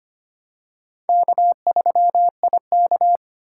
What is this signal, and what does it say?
Morse code, heard as a keyed tone, about 25 wpm, 700 Hz. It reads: K3IK
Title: USA Call Signs - 4 Characters - 4x Character Spacing - 25wpm